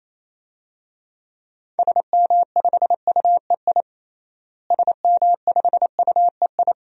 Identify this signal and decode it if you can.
Morse code, heard as a keyed tone, about 28 wpm, 700 Hz. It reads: SM5UEI SM5UEI
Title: DXCC-100 Call Signs - 6 Characters - Repeated 2x - 28wpm